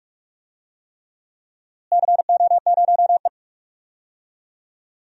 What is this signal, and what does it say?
Morse code, heard as a keyed tone, about 46 wpm, 700 Hz. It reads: CO0E
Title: DXCC-100 Call Signs - 4 Characters - Repeated 3x - 45wpm